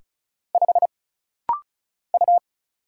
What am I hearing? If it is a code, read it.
Morse code, heard as a keyed tone, about 36 wpm, 700 Hz. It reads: 5 U
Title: Single Letter-Number - Rapid-Fire 35wpm